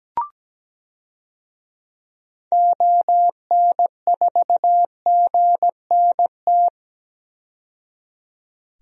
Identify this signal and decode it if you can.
Morse code, heard as a keyed tone, about 17 wpm, 700 Hz. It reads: ON4GNT
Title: DXCC-100 Call Signs - 6 Characters - 17wpm